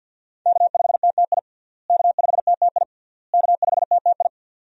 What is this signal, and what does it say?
Morse code, heard as a keyed tone, about 47 wpm, 700 Hz. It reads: K5TTI K5TTI K5TTI
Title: USA Call Signs - 5 Characters - Repeated 3x - 50wpm